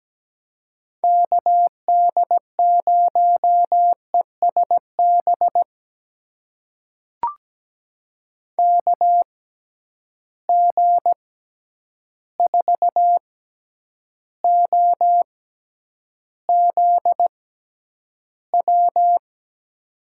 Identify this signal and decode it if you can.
Morse code, heard as a keyed tone, about 17 wpm, spaced as for 9 wpm, 700 Hz. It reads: KD0ESB KG4OZW
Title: USA Call Signs - 6 Characters - 6x Character Spacing - 17wpm